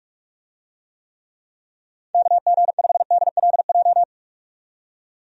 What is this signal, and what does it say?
Morse code, heard as a keyed tone, about 43 wpm, 700 Hz. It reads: KG5DLJ
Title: USA Call Signs - 6 Characters - 45wpm